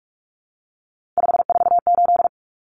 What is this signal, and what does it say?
Morse code, heard as a keyed tone, about 44 wpm, 700 Hz. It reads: 548
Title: Numbers - 3-Digit 45wpm